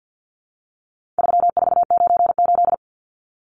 Morse code, heard as a keyed tone, about 50 wpm, 700 Hz. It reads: 3498
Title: Numbers - 4-Digit 50wpm